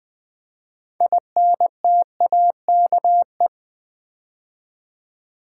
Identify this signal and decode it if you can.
Morse code, heard as a keyed tone, about 20 wpm, 700 Hz. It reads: INTAKE
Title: 6 Letter Words - First 3 Letters Given - 20wpm